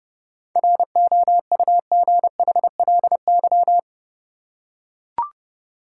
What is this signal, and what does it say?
Morse code, heard as a keyed tone, about 30 wpm, 700 Hz. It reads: ROUGHLY
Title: Seven Letter Words - 30wpm